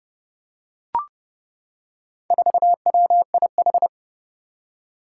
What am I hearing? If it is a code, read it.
Morse code, heard as a keyed tone, about 30 wpm, 700 Hz. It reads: 4WIH
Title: Four Challenging Letter-Number Combinations 30wpm